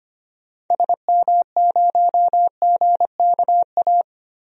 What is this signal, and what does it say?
Morse code, heard as a keyed tone, about 25 wpm, 700 Hz. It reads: SM0GKA
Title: DXCC-100 Call Signs - 6 Characters - Repeated 3x - 25wpm